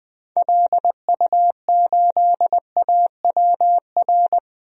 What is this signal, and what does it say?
Morse code, heard as a keyed tone, about 20 wpm, 700 Hz. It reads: LU8AWR
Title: DXCC-100 Call Signs - 6 Characters - 20wpm